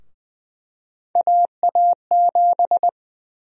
Morse code, heard as a keyed tone, about 20 wpm, 700 Hz. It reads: AA7
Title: USA Call Sign Prefixes - 20wpm